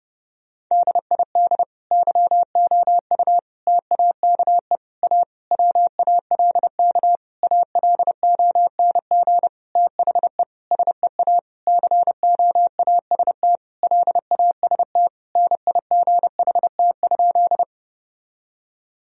Morse code, heard as a keyed tone, about 32 wpm, 700 Hz. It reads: DID YOU TAKE A WALK ALONG THE SEA COAST LAST NIGHT?
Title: Sentences from Top 1000 Words - Review Part 4 - 30wpm